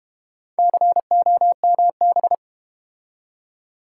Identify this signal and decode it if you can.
Morse code, heard as a keyed tone, about 30 wpm, 700 Hz. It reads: COMB
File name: Easy Words - Speed-Racing with Hints - 25wpm